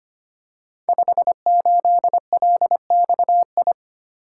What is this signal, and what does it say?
Morse code, heard as a keyed tone, about 25 wpm, 700 Hz. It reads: 58LXI
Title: Code Groups - 5 Letter:Numbers - 25wpm